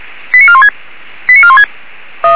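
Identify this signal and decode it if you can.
Five-tone selective call (ZVEI), heard as quick runs of five tones, tones of about 70 ms. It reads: 89416 89416
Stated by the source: ZVEI five-tone selective call received by SDR.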